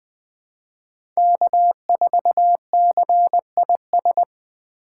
Morse code, heard as a keyed tone, about 20 wpm, 700 Hz. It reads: K4CIS
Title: USA Call Signs - 5 Characters - 4x Character Spacing - 20wpm